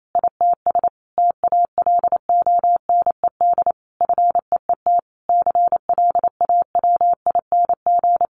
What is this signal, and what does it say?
Morse code, heard as a keyed tone, about 28 wpm, 700 Hz. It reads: ITS TALONED FEET CLAWING
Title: A Sound Of Thunder 28wpm